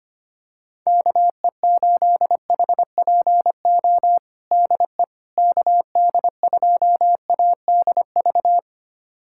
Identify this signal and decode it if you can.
Morse code, heard as a keyed tone, about 25 wpm, 700 Hz. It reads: KE8HPO DE KD2ADV
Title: USA Callsign DE Callsign - 6 Characters - 25wpm